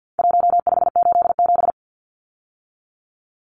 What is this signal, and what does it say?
Morse code, heard as a keyed tone, about 50 wpm, 700 Hz. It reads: 1587
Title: Numbers - Contest Style - 4-Digit 50wpm